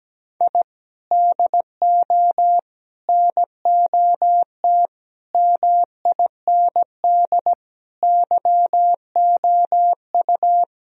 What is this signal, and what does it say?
Morse code, heard as a keyed tone, about 17 wpm, 700 Hz. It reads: I DO NOT MIND YOU
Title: Sentences from Top 1000 Words - Review Part 3 - 17wpm